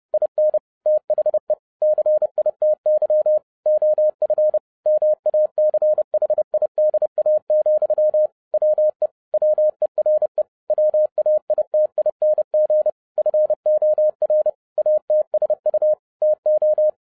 Morse code, heard as a keyed tone, about 30 wpm, 610 Hz. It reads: IN THE CITY OF MACHIDA, WE WERE WAITING FOR ATSU TO